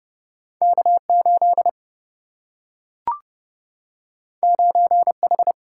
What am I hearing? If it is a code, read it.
Morse code, heard as a keyed tone, about 30 wpm, 700 Hz. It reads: K 8 9 H
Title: Single-Letter Single-Number 30wpm